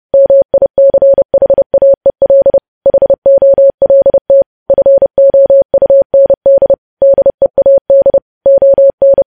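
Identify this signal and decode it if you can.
Morse code, heard as a keyed tone, about 30 wpm, 550 Hz. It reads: MICHAEL HOLT FOUND DEAD ON
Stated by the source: News headlines sent in Morse code at 30.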